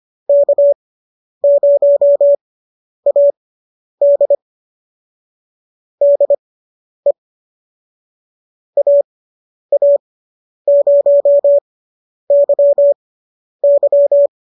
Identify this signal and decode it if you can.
Morse code, heard as a keyed tone, about 25 wpm, spaced as for 10 wpm, 590 Hz. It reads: K0AD DE AA0YY